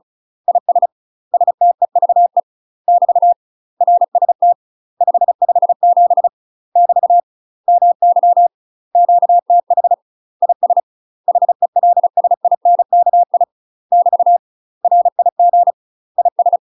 Morse code, heard as a keyed tone, about 34 wpm, 700 Hz. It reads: IS STEVE <BT> RST 557 <BT> MY QTH IS HELSINKI <BT> RIG IS
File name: UK MASTER QSOs TRAINING / 35WPM / UK MASTER QSOs 0043